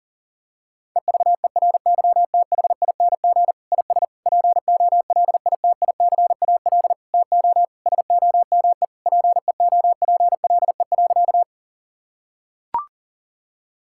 Morse code, heard as a keyed tone, about 40 wpm, 700 Hz. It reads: EVERYTHING IS POLITICAL TO SOME PEOPLE.
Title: Sentences from Top 1000 Words - Review Part 5 - 40wpm